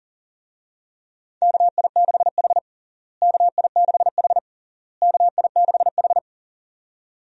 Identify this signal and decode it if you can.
Morse code, heard as a keyed tone, about 40 wpm, 700 Hz. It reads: KI6H KI6H KI6H
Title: USA Call Signs - 4 Characters - Repeated 3x - 40wpm